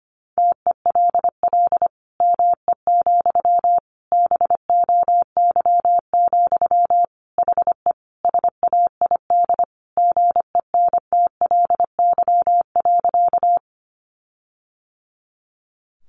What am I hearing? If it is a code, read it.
Morse code, heard as a keyed tone, about 25 wpm, 700 Hz. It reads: TELL ME, BOY, HE SAID GENTLY.
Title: Sentences from Top 1500 Words 25wpm